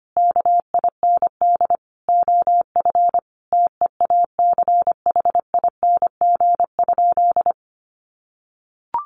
Morse code, heard as a keyed tone, about 25 wpm, 700 Hz. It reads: KIND OF TEACHING?
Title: Sentences from Top 1100 Words 25wpm